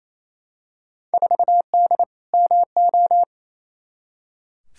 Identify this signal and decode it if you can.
Morse code, heard as a keyed tone, about 28 wpm, 700 Hz. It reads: 4D MO
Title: Field Day Exchanges Encore - 28wpm